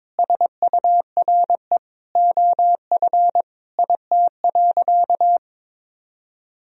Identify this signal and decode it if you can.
Morse code, heard as a keyed tone, about 21 wpm, 700 Hz. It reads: SURE OF IT.
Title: Sentences from Top 1000 Words - Review Part 2 - 22wpm